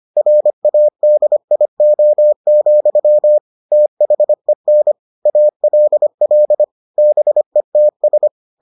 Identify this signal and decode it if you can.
Morse code, heard as a keyed tone, about 25 wpm, 600 Hz. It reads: RADIO, THEN ALL BETS